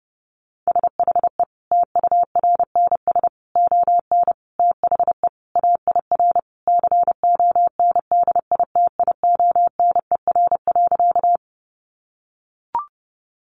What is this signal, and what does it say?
Morse code, heard as a keyed tone, about 30 wpm, 700 Hz. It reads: SHE TURNS ON THE AIR CONDITIONER.
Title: Additional Easy Phrases 30wpm